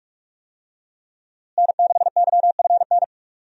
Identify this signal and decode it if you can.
Morse code, heard as a keyed tone, about 46 wpm, 700 Hz. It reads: N6YFN
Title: USA Call Signs - 5 Characters - Repeated 3x - 45wpm